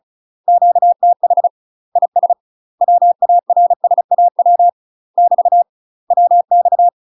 Morse code, heard as a keyed tone, about 36 wpm, 700 Hz. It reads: QTH IS WARSAW <BT> WX
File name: UK MASTER QSOs TRAINING / 35WPM / UK MASTER QSOs 0038